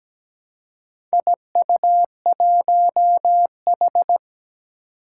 Morse code, heard as a keyed tone, about 17 wpm, 700 Hz. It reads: IU1H
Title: Four Challenging Letter-Number Combinations 17wpm